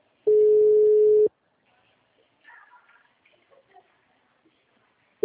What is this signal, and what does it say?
Ringback tone (Europe, 425 Hz).